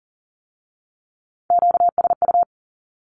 Morse code, heard as a keyed tone, about 42 wpm, 700 Hz. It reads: QSU
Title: Q-Codes with Spoken Meaning - 40wpm